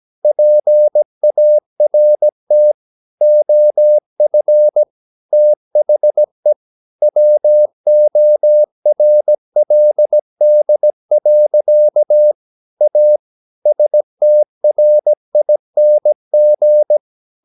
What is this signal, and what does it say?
Morse code, heard as a keyed tone, about 17 wpm, 600 Hz. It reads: PART OF THE WORLD. A STRING
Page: Romanen Never af Ken Follet på engelsk.